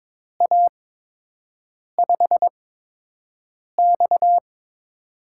Morse code, heard as a keyed tone, about 22 wpm, spaced as for 6 wpm, 700 Hz. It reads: A5X
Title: USA Call Signs - 8x Character Spacing - 22wpm